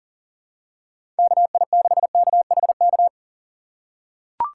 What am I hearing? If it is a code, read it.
Morse code, heard as a keyed tone, about 40 wpm, 700 Hz. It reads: KI6KHK
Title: USA Call Signs - 6 Characters - 6x Character Spacing - 40wpm